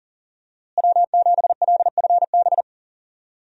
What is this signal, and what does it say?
Morse code, heard as a keyed tone, about 40 wpm, 700 Hz. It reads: W7LFB
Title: USA Call Signs - 5 Characters - Repeated 3x - 40wpm